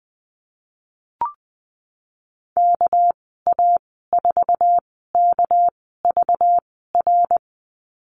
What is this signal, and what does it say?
Morse code, heard as a keyed tone, about 20 wpm, spaced as for 15 wpm, 700 Hz. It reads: KA4KVR
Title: 219 Lesson 35g - Number 0 - Call Signs 20wpm